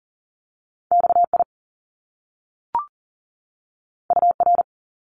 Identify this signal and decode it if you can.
Morse code, heard as a keyed tone, about 40 wpm, 700 Hz. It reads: XI UR
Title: Top Two Letter Combinations in Words 40wpm